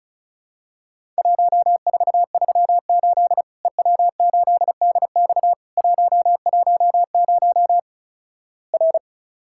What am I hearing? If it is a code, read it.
Morse code, heard as a keyed tone, about 36 wpm, 700 Hz. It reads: 1438 EW8DX 110 R
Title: Worked All Europe DX Contest - QTC Exchange - 35wpm